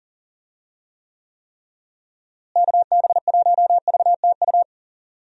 Morse code, heard as a keyed tone, about 37 wpm, 700 Hz. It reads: KB1VTU
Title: USA Call Signs - 6 Characters - Repeated 3x - 40wpm